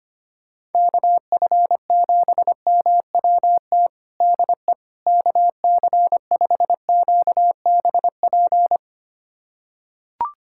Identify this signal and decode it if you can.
Morse code, heard as a keyed tone, about 25 wpm, 700 Hz. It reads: KF7MWT DE KC5QBP
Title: USA Callsign DE Callsign - 6 Characters - 25wpm